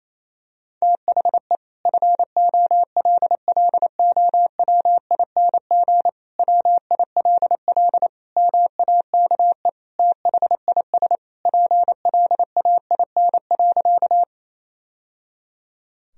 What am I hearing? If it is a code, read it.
Morse code, heard as a keyed tone, about 28 wpm, 700 Hz. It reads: THE FOLLOWING WILL MAKE THIS PLAIN.
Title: Sentences from Top 1000 Words - Review Part 4 - 28wpm